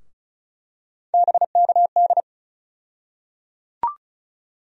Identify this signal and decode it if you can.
Morse code, heard as a keyed tone, about 35 wpm, 700 Hz. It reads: BKD
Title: USA Call Sign Postfixes - 3 Letters 35wpm